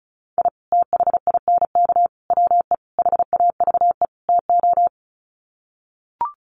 Morse code, heard as a keyed tone, about 34 wpm, 700 Hz. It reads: I THINK WE HAVE TO
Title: Sets of 5 Words 35wpm